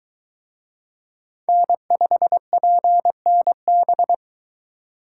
Morse code, heard as a keyed tone, about 23 wpm, 700 Hz. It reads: N5PNB